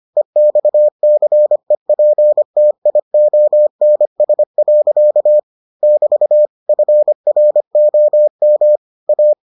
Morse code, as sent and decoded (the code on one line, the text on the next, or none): . -..- -.-. .--. - .. --- -. ... .-.-.- / -...- / ..-. .-. --- -- / .-
EXCEPTIONS. <BT> FROM A